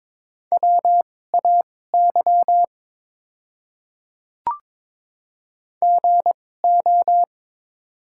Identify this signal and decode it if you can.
Morse code, heard as a keyed tone, about 21 wpm, spaced as for 16 wpm, 700 Hz. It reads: WAY GO
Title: 2x Character Spacing - Top 100 Words - 22wpm